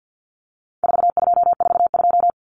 Morse code, heard as a keyed tone, about 50 wpm, 700 Hz. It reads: 4242
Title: Numbers - 4-Digit 50wpm